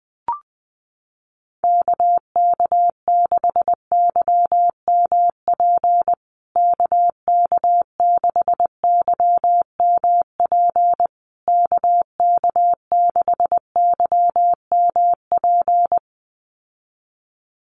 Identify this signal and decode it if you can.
Morse code, heard as a keyed tone, about 20 wpm, 700 Hz. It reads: KK6YMP KK6YMP KK6YMP
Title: USA Call Signs Repeated 3x 20wpm